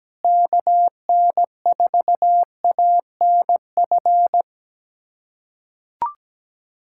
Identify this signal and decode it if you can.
Morse code, heard as a keyed tone, about 17 wpm, 700 Hz. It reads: KN4ANF
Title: USA Callsign DE Callsign - 6 Characters - 17wpm